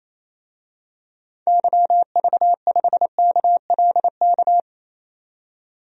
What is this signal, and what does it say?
Morse code, heard as a keyed tone, about 27 wpm, 700 Hz. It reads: YV5KLK
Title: DXCC-100 Call Signs - 6 Characters - 28wpm